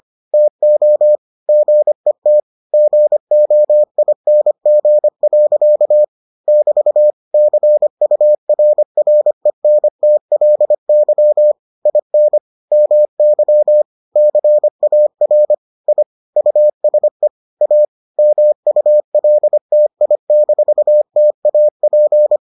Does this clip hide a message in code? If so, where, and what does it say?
Yes: Morse code from 0.3 to 22.4 s, heard as a keyed tone, about 25 wpm, 600 Hz. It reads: TO GET GOING. <BT> CURRENTLY IN MY CAR I USE A MULTI-TAP